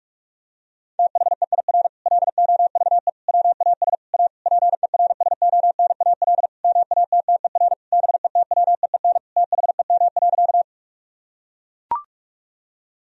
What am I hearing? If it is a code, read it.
Morse code, heard as a keyed tone, about 43 wpm, 700 Hz. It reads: THEIR LOVE WAS A PERSONAL MATTER BETWEEN THEM.
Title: Sentences from Top 1000 Words - Review Part 4 - 45wpm